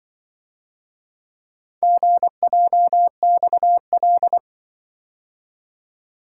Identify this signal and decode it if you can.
Morse code, heard as a keyed tone, about 25 wpm, 700 Hz. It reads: GJXL